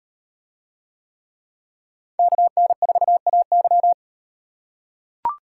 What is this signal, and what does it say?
Morse code, heard as a keyed tone, about 38 wpm, 700 Hz. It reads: KN4AY